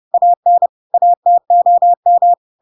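Morse code, heard as a keyed tone, about 30 wpm, 700 Hz. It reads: AN ATOM
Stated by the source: Chapter 2 of The Adventures of Pinocchio read as a book in Morse code (CW) at 30 wpm, by ik1hge.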